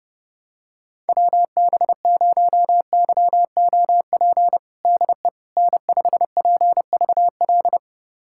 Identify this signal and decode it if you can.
Morse code, heard as a keyed tone, about 30 wpm, 700 Hz. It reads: WB0YOP DE N5PVL
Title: USA Callsign DE Callsign 30wpm